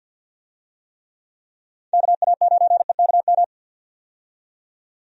Morse code, heard as a keyed tone, about 52 wpm, 700 Hz. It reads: KA9EKM
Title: USA Call Signs - 6 Characters - 50wpm